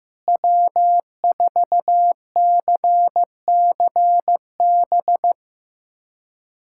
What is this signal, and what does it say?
Morse code, heard as a keyed tone, about 15 wpm, 700 Hz. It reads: W4CCB
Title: USA Callsign DE Callsign - 5 Characters - 15wpm